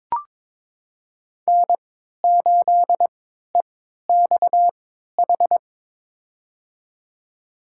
Morse code, heard as a keyed tone, about 22 wpm, spaced as for 13 wpm, 700 Hz. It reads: N8EXH
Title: USA Call Signs - 3x Character Spacing - 22wpm